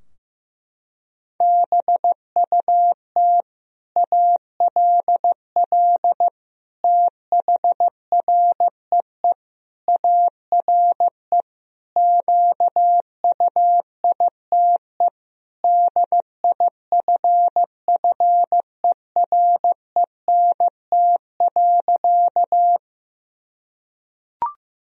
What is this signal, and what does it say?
Morse code, heard as a keyed tone, about 15 wpm, 700 Hz. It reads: BUT ALL THREE ARE QUITE DIFFERENT.
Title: Sentences from Top 1000 Words - Review Part 3 - 15wpm